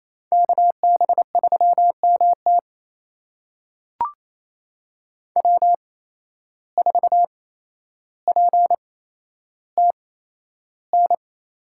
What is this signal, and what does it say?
Morse code, heard as a keyed tone, about 28 wpm, spaced as for 12 wpm, 700 Hz. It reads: KB3MT W4PTN
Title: USA Call Signs - 5 Characters - 8x Character Spacing - 28wpm